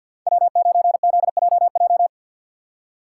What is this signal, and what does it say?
Morse code, heard as a keyed tone, about 47 wpm, 700 Hz. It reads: W9ZJJ
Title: USA Call Signs - 5 Characters - 3x Character Spacing - 50wpm